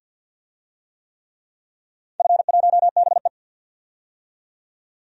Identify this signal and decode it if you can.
Morse code, heard as a keyed tone, about 47 wpm, 700 Hz. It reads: F1BE
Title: DXCC-100 Call Signs - 4 Characters - 50wpm